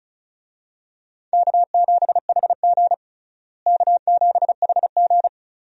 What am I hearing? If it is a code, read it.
Morse code, heard as a keyed tone, about 36 wpm, 700 Hz. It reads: K7HG K7HG
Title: USA Call Signs - 4 Characters - Repeated 2x - 35wpm